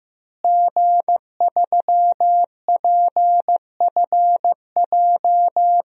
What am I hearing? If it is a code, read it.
Morse code, heard as a keyed tone, about 15 wpm, 700 Hz. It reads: G3PFJ
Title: DXCC-100 Call Signs - 5 Characters - Repeated 2x - 15wpm